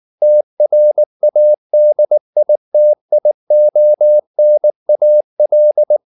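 Morse code, heard as a keyed tone, about 19 wpm, 600 Hz. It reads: TRADITIONAL
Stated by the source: Morse code at 19 wpm from CW med Gnister, reading Ken Follett's novel Never in English.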